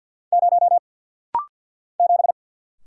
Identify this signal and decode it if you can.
Morse code, heard as a keyed tone, about 48 wpm, 700 Hz. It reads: 0 7
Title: Single Letter-Number - Rapid-Fire 50wpm